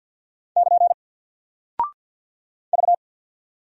Morse code, heard as a keyed tone, about 48 wpm, 700 Hz. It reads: <KN> V
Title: Single Character (Letter:Number:Punctuation:Prosign) Alt 50wpm